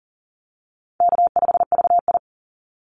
Morse code, heard as a keyed tone, about 40 wpm, 700 Hz. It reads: K5VI
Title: Easy Call Signs Repeated 3x - 40wpm